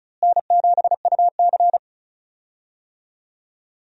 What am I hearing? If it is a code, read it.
Morse code, heard as a keyed tone, about 36 wpm, 700 Hz. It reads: N7UC
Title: USA Call Signs - 4 Characters - Repeated 2x - 35wpm